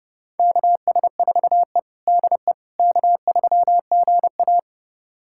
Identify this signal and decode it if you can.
Morse code, heard as a keyed tone, about 32 wpm, 700 Hz. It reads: KS4E DE K3GA
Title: USA Callsign DE Callsign - 4 Characters - 30wpm